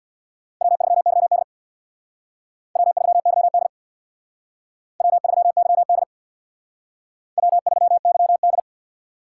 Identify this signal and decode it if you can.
Morse code, heard as a keyed tone, about 62 wpm, 700 Hz. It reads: W3YD W3YD W3YD W3YD
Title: Speed Racing - USA Call Signs 50wpm